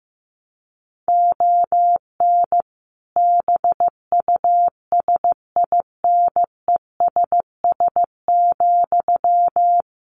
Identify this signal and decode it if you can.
Morse code, heard as a keyed tone, about 15 wpm, 700 Hz. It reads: ON BUSINESS,
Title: Sentences from Top 1300 Words 15wpm